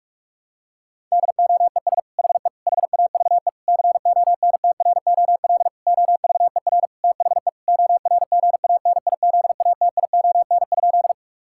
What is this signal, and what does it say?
Morse code, heard as a keyed tone, about 41 wpm, 700 Hz. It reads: DOES HE HAVE CONTROL OVER THE ORGANIZATION?
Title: Sentences from Top 1000 Words - Review Part 5 - 45wpm